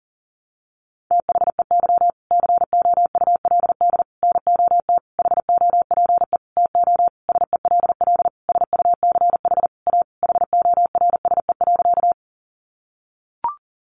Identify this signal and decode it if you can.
Morse code, heard as a keyed tone, about 40 wpm, 700 Hz. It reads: THEY COULD NOT HOPE TO SELL SUCH A HORSE.
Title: Sentences from Top 1100 Words 40wpm